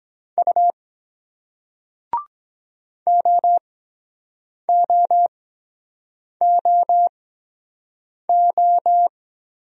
Morse code, heard as a keyed tone, about 23 wpm, spaced as for 7 wpm, 700 Hz. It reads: U OOOO